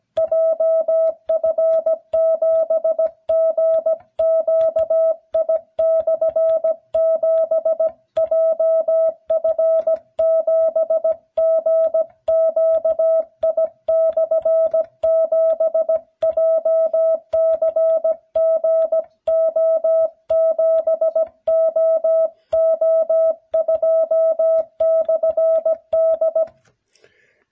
Morse code, heard as a keyed tone, about 16 wpm, 650 Hz. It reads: JF7GQI/7 JF7GQI/7 JCGO7OO2/D